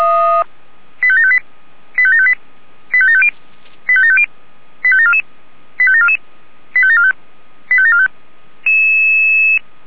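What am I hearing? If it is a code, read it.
Five-tone selective call (ZVEI), heard as quick runs of five tones, tones of about 70 ms. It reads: 85758 85758 85750 85750 85744 85744 85745 85745
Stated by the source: Five-tone ZVEI selective call received on SDR.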